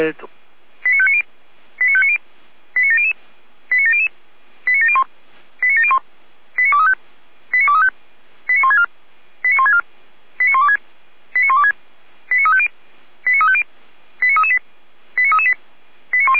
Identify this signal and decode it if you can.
Five-tone selective call (ZVEI), heard as quick runs of five tones, tones of about 80 ms, with speech. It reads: 89590 89590 89700 89700 89701 89701 89235 89235 89164 89164 89126 89126 89350 89350 89338 89338